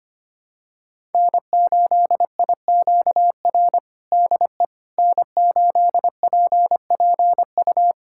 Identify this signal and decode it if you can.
Morse code, heard as a keyed tone, about 25 wpm, 700 Hz. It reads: N8IQR DE N8PPU
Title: USA Callsign DE Callsign - 5 Characters - 25wpm